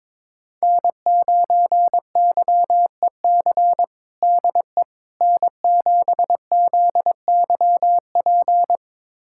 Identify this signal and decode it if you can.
Morse code, heard as a keyed tone, about 22 wpm, 700 Hz. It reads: N9YEC DE N7ZYP
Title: USA Callsign DE Callsign - 5 Characters - 22wpm